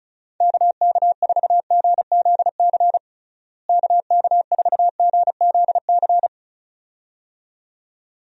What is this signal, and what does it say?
Morse code, heard as a keyed tone, about 36 wpm, 700 Hz. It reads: KK4GZC KK4GZC
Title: USA Call Signs - 6 Characters - Repeated 2x - 35wpm